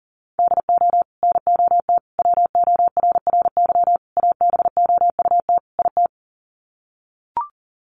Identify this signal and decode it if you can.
Morse code, heard as a keyed tone, about 40 wpm, 700 Hz. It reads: DO NOT WORRY ABOUT IT
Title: Sets of 5 Words 40wpm